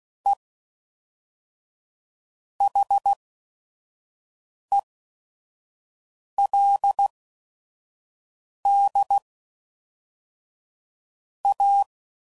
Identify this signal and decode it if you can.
Morse code, heard as a keyed tone, about 16 wpm, spaced as for 5 wpm, 800 Hz. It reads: E HELD A
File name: CW Code Groups @5WPM / QSOs / kc7sac